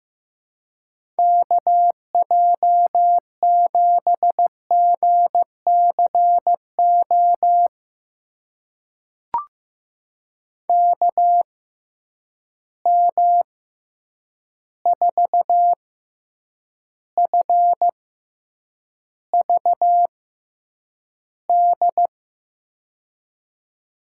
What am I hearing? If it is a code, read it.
Morse code, heard as a keyed tone, about 15 wpm, spaced as for 8 wpm, 700 Hz. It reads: KJ7GCO KM4FVD
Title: USA Call Signs - 6 Characters - 6x Character Spacing - 15wpm